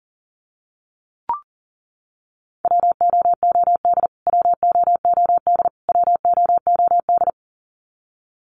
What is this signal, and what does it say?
Morse code, heard as a keyed tone, about 40 wpm, 700 Hz. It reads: WOOD WOOD WOOD
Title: Top 300 Words Repeated 40wpm